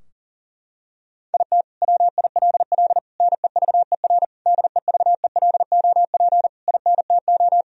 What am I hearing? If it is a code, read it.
Morse code, heard as a keyed tone, about 37 wpm, 700 Hz. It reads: IT WILL NEVER DEVELOP INTO
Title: Sentences from Top 1000 Words - Review Part 2 - 40wpm